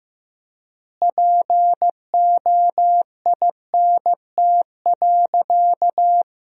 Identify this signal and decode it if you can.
Morse code, heard as a keyed tone, about 16 wpm, 700 Hz. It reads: POINT.
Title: 3x Word-Spacing - Sentences from Top 200 Words - 15wpm